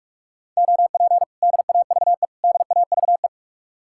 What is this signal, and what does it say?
Morse code, heard as a keyed tone, about 44 wpm, 700 Hz. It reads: OP DAVE DAVE